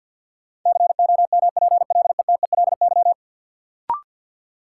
Morse code, heard as a keyed tone, about 49 wpm, 700 Hz. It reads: COMPLETELY
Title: Sentences from Top 1100 Words - New Word Drill Alt - 50wpm